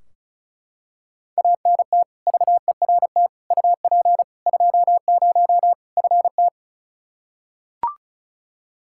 Morse code, heard as a keyed tone, about 36 wpm, 700 Hz. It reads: ANT VERT UP 20 FT
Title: K1USN SST QSO Practice 35wpm